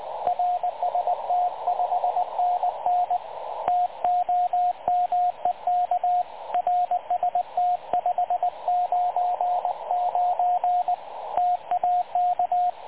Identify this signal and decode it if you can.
Morse code, heard as a keyed tone, about 20 wpm, 720 Hz, with radio noise. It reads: RST 5NN TOMEK RST 599 TAK
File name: dlazaawansowanych / nagraniazpasma / websdr recording 3567.0kHz